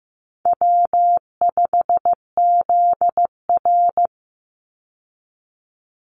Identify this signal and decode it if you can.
Morse code, heard as a keyed tone, about 15 wpm, 700 Hz. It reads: W5ZR
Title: Easy Call Signs Repeated 2x - 15wpm